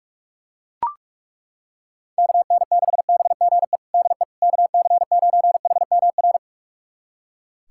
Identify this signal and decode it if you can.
Morse code, heard as a keyed tone, about 42 wpm, 700 Hz. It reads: KN6BGE DE KC9HMR
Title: USA Callsign DE Callsign 45wpm